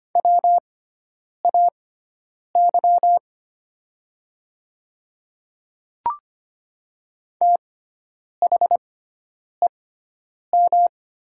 Morse code, heard as a keyed tone, about 25 wpm, spaced as for 9 wpm, 700 Hz. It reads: WAY THEM
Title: 6x Character Spacing - Top 100 Words - 25wpm